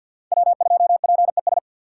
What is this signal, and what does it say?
Morse code, heard as a keyed tone, about 64 wpm, 700 Hz. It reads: W2PES